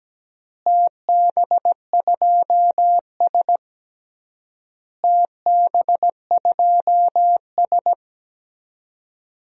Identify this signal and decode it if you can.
Morse code, heard as a keyed tone, about 17 wpm, 700 Hz. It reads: TB2S TB2S
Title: DXCC-100 Call Signs - 4 Characters - Repeated 3x - 17wpm